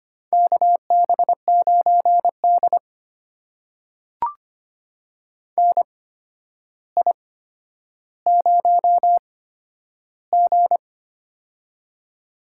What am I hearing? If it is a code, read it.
Morse code, heard as a keyed tone, about 25 wpm, spaced as for 11 wpm, 700 Hz. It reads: KB9D NI0G
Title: USA Call Signs - 4 Characters - 8x Character Spacing - 25wpm